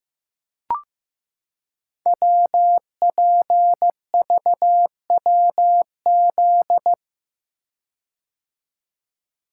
Morse code, heard as a keyed tone, about 15 wpm, 700 Hz. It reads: WPVWZ
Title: Code Groups - 5 Letters - 15wpm